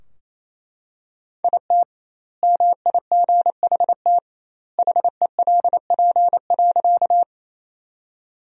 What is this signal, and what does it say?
Morse code, heard as a keyed tone, about 28 wpm, 700 Hz. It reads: IT MIGHT HELP.
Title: 2x Word-Spacing - Sentences from Top 300 Words - 28wpm